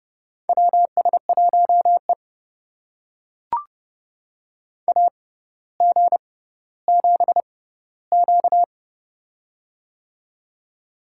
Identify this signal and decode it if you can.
Morse code, heard as a keyed tone, about 30 wpm, spaced as for 15 wpm, 700 Hz. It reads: WS1E AG7Q
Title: USA Call Signs - 4 Characters - 6x Character Spacing - 30wpm